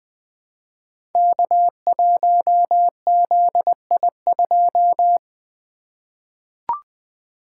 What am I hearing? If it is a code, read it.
Morse code, heard as a keyed tone, about 20 wpm, 700 Hz. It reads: K1ZI2
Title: Code Groups - 5 Letter:Numbers - 20wpm